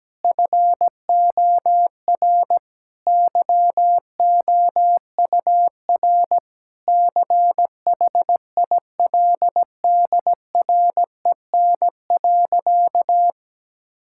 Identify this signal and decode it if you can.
Morse code, heard as a keyed tone, about 17 wpm, 700 Hz. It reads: FOR YOUR CHILDREN.
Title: Sentences from Top 1000 Words - Review Part 6 - 17wpm